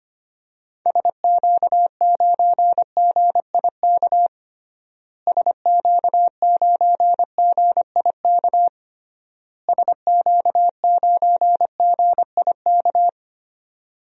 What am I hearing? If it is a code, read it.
Morse code, heard as a keyed tone, about 25 wpm, 700 Hz. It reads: SQ9GIK SQ9GIK SQ9GIK
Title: DXCC-100 Call Signs - 6 Characters - Repeated 3x - 25wpm